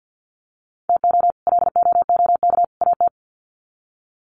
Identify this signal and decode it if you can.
Morse code, heard as a keyed tone, about 49 wpm, 700 Hz. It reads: TO LOOK AT
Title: Sets of 3 Words 50wpm